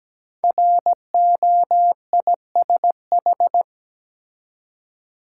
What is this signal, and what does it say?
Morse code, heard as a keyed tone, about 17 wpm, 700 Hz. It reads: ROISH
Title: Code Groups - 5 Letters - 17wpm